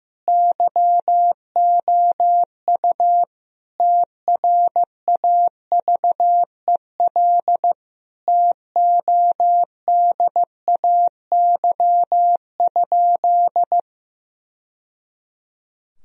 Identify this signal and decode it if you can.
Morse code, heard as a keyed tone, about 15 wpm, 700 Hz. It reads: YOU TRAVEL TODAY?
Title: Sentences from Top 1000 Words - Review Part 2 - 15wpm